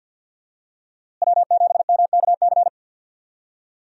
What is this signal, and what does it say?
Morse code, heard as a keyed tone, about 46 wpm, 700 Hz. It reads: W7MKC
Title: USA Call Signs - 5 Characters - Repeated 3x - 50wpm